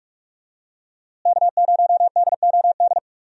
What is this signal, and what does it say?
Morse code, heard as a keyed tone, about 46 wpm, 700 Hz. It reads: K0DOD
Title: USA Call Signs - 5 Characters - Repeated 3x - 45wpm